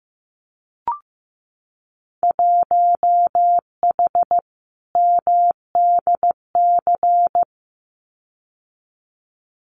Morse code, heard as a keyed tone, about 15 wpm, 700 Hz. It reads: 1H MDC
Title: Winter Field Day Exchanges 15wpm